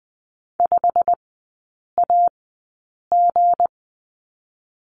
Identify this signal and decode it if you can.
Morse code, heard as a keyed tone, about 20 wpm, 700 Hz. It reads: 5 A G
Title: 184 Lesson 30e - Letter b - Three Characters 20wpm